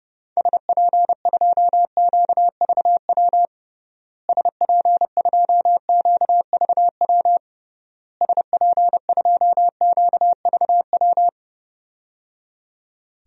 Morse code, heard as a keyed tone, about 32 wpm, 700 Hz. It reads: SP2QVW SP2QVW SP2QVW
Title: DXCC-100 Call Signs - 6 Characters - Repeated 3x - 30wpm